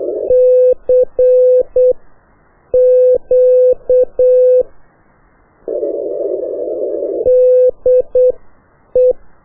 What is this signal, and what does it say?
Morse code, heard as a keyed tone, about 8 wpm, spaced as for 6 wpm, 500 Hz. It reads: CQ DE
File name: qso1-qrs